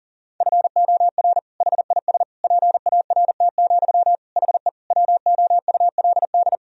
Morse code, heard as a keyed tone, about 40 wpm, 700 Hz. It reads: FOR HIS PART, HE WOULD
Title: Sentences from Top 1000 Words - Review Part 2 - 40wpm